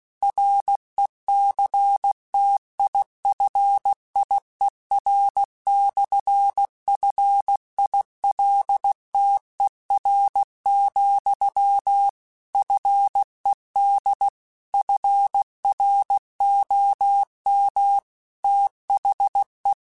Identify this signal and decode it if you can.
Morse code, heard as a keyed tone, about 16 wpm, 800 Hz. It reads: RECTIFIER/FILTER, FED FROM THE
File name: Technical Subjects / Technical Subjects-mp3s / Sw Pwr Sup @15